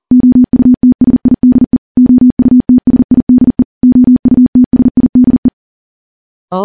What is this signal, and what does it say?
Morse code, heard as a keyed tone, about 40 wpm, 250 Hz. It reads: OUTSIDE OUTSIDE OUTSIDE
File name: morse / morse-training-40WPM